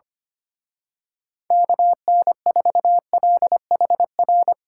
Morse code, heard as a keyed tone, about 26 wpm, 700 Hz. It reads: KN4LHR
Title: Speed Racing - USA Call Signs 25wpm